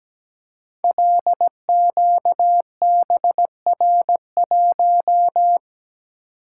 Morse code, heard as a keyed tone, about 17 wpm, 700 Hz. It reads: LQBR1
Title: Code Groups - 5 Letter:Numbers - 17wpm